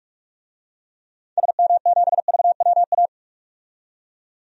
Morse code, heard as a keyed tone, about 47 wpm, 700 Hz. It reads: SM7VWA
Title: DXCC-100 Call Signs - 6 Characters - 45wpm